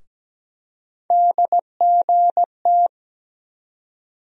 Morse code, heard as a keyed tone, about 17 wpm, 700 Hz. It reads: DGT
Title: USA Call Sign Postfixes - 3 Letters 17wpm